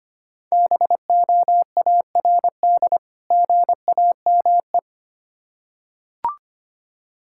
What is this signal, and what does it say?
Morse code, heard as a keyed tone, about 25 wpm, 700 Hz. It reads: BOARD GAME
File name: practice / 25wpm / Difficult Words with Hints - 25wpm